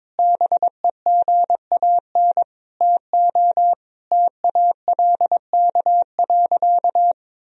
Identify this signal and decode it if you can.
Morse code, heard as a keyed tone, about 21 wpm, 700 Hz. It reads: BEGAN TO TALK.
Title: Sentences from Top 1000 Words - Review Part 1 - 22wpm